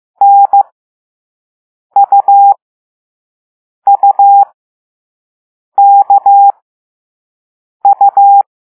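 Morse code, heard as a keyed tone, about 15 wpm, spaced as for 5 wpm, 800 Hz. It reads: NUUKU
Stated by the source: Morse code lesson 5 drill of random five-letter groups using all letters learned so far.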